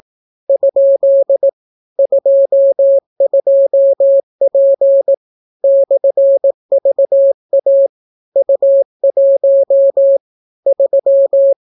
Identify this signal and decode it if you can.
Morse code, heard as a keyed tone, about 18 wpm, 550 Hz. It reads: ? 22P /VA U1 3